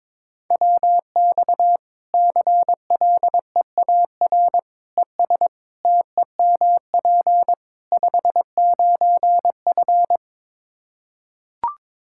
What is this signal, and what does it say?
Morse code, heard as a keyed tone, about 22 wpm, 700 Hz. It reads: WX CLEAR ES TEMP 59F
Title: Practice QSO - Exchange 2 - Info - 22wpm